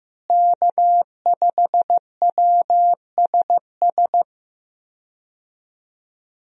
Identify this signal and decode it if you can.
Morse code, heard as a keyed tone, about 15 wpm, 700 Hz. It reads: K5WSS
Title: USA Call Signs - 5 Characters - Repeated 2x - 15wpm